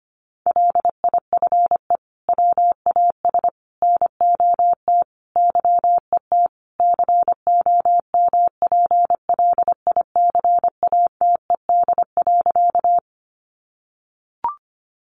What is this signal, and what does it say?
Morse code, heard as a keyed tone, about 25 wpm, 700 Hz. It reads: LIFE WAS NOT YET COMPLICATED.
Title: Sentences from Top 1200 Words 25wpm